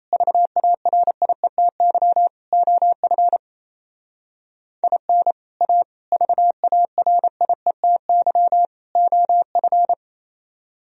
Morse code, heard as a keyed tone, about 35 wpm, 700 Hz. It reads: VARIETY OF IN A VARIETY OF
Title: Speed-Racing - Sets of 4 Words 28wpm